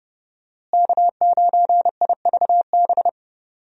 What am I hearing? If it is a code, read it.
Morse code, heard as a keyed tone, about 31 wpm, 700 Hz. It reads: K9IVB
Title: USA Call Signs - 5 Characters - Repeated 3x - 30wpm